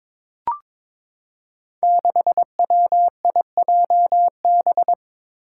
Morse code, heard as a keyed tone, about 23 wpm, 700 Hz. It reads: 6WIJB
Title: Five Challenging Letter-Number Combinations 22wpm